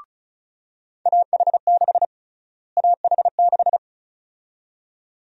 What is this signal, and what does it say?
Morse code, heard as a keyed tone, about 36 wpm, 700 Hz. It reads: AH6 AH6
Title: USA Call Sign Prefixes - Repeated 2x - 35wpm